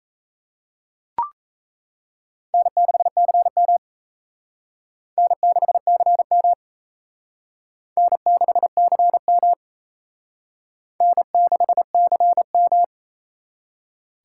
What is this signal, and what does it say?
Morse code, heard as a keyed tone, about 35 wpm, 700 Hz. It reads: N6CM N6CM N6CM N6CM
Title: Speed Racing - USA 1x2 Call Signs 28wpm